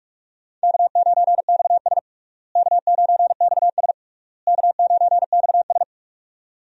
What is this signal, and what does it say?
Morse code, heard as a keyed tone, about 46 wpm, 700 Hz. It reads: K9XS K9XS K9XS
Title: USA Call Signs - 4 Characters - Repeated 3x - 45wpm